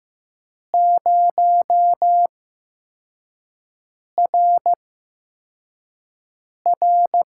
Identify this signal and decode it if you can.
Morse code, heard as a keyed tone, about 15 wpm, spaced as for 4 wpm, 700 Hz. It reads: 0RR
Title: USA Call Signs - 5 Characters - 8x Character Spacing - 15wpm